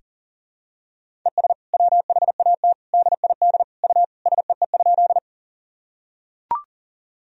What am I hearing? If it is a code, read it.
Morse code, heard as a keyed tone, about 40 wpm, 700 Hz. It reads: ES WHAT DID U SEE?
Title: Sentences from Top 100 Words and Abbreviations 40wpm